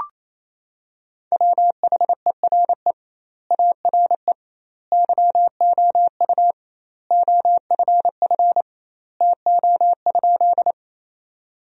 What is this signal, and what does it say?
Morse code, heard as a keyed tone, about 27 wpm, 700 Hz. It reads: WHERE ARE YOU OFF TO?
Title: 2x Word-Spacing - Sentences from Top 200 Words - 28wpm